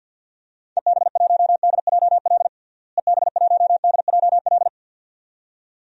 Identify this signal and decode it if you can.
Morse code, heard as a keyed tone, about 47 wpm, 700 Hz. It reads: EB1DJL EB1DJL
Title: DXCC-100 Call Signs - 6 Characters - Repeated 2x - 50wpm